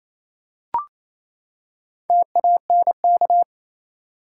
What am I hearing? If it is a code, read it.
Morse code, heard as a keyed tone, about 28 wpm, 700 Hz. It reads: TANK
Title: Four Letter Words - 28wpm